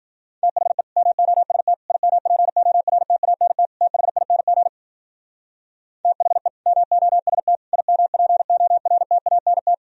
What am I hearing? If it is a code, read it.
Morse code, heard as a keyed tone, about 51 wpm, 700 Hz. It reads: THE MOST IMPORTANT THING THE MOST IMPORTANT